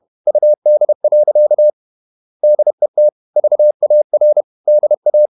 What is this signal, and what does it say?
Morse code, heard as a keyed tone, about 31 wpm, 600 Hz. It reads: UD. DET VAR DA
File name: Ferie QRQ 31wpm